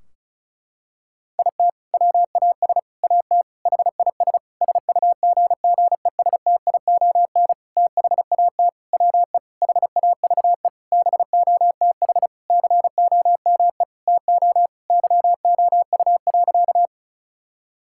Morse code, heard as a keyed tone, about 36 wpm, 700 Hz. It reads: IT WAS AT HIS SUGGESTION THAT WE HAVE BOTH COME TO YOU.
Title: Sentences from Top 1600 Words 35wpm